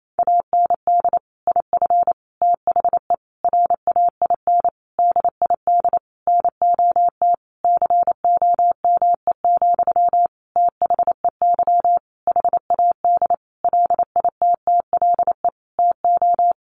Morse code, heard as a keyed tone, about 28 wpm, 700 Hz. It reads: AND IF THE RAIN DID NOT COME, THEY HAD LITTLE TO
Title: Sentences from Top 1300 Words 28wpm